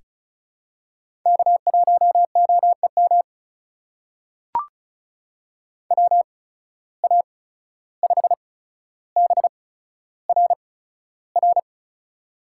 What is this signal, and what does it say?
Morse code, heard as a keyed tone, about 36 wpm, spaced as for 14 wpm, 700 Hz. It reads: K1OEM WA5BRR
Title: USA Call Signs - 8x Character Spacing - 35wpm